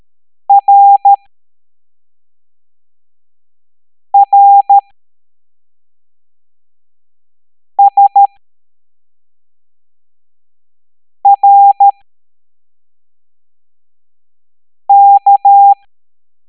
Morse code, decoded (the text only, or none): RRSRK